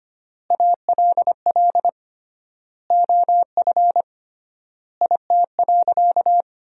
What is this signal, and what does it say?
Morse code, heard as a keyed tone, about 25 wpm, 700 Hz. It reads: ALL OF IT.
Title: 3x Word-Spacing - Sentences from Top 500 Words - 25wpm